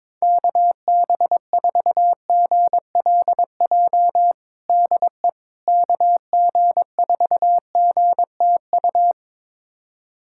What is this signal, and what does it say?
Morse code, heard as a keyed tone, about 21 wpm, 700 Hz. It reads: KB4GLJ DE KG4GTU
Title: USA Callsign DE Callsign - 6 Characters - 22wpm